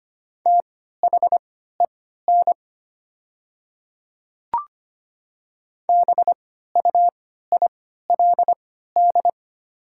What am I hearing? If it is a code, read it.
Morse code, heard as a keyed tone, about 25 wpm, spaced as for 14 wpm, 700 Hz. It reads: THEN BUILD
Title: 3x Character Spacing - Top 200 Words - 25wpm